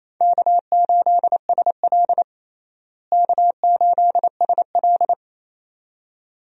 Morse code, heard as a keyed tone, about 28 wpm, 700 Hz. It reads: K8SL K8SL
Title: USA Call Signs - 4 Characters - Repeated 2x - 28wpm